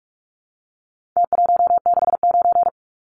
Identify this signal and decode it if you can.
Morse code, heard as a keyed tone, about 44 wpm, 700 Hz. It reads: T169
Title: Numbers - Contest Style - 4-Digit 45wpm